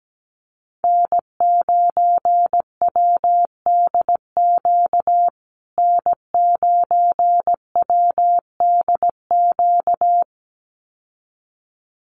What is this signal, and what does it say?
Morse code, heard as a keyed tone, about 17 wpm, 700 Hz. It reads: N9WDQ N9WDQ
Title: USA Call Signs Repeated 2x 17wpm